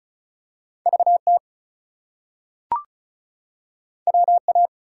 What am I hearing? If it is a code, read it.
Morse code, heard as a keyed tone, about 35 wpm, 700 Hz. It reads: VT WA
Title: US States and Canadian Provinces 35wpm